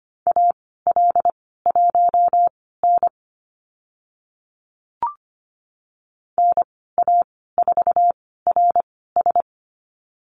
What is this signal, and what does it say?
Morse code, heard as a keyed tone, about 25 wpm, spaced as for 16 wpm, 700 Hz. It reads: AL1N NA4RS
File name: Lesson 15g - Number 5 - Call Signs 25wpm